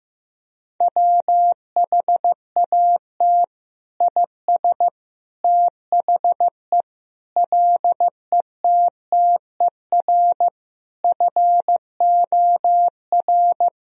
Morse code, heard as a keyed tone, about 16 wpm, 700 Hz. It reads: WHAT IS THE LETTER FOR
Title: Sentences from Top 1000 Words - Review Part 2 - 15wpm